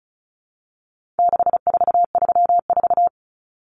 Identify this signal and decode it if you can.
Morse code, heard as a keyed tone, about 34 wpm, 700 Hz. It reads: 6434
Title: Numbers - 4-Digit 35wpm